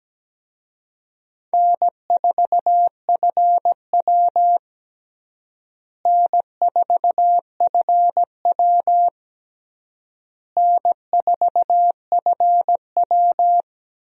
Morse code, heard as a keyed tone, about 17 wpm, 700 Hz. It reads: N4FW N4FW N4FW
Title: USA Call Signs - 4 Characters - Repeated 3x - 17wpm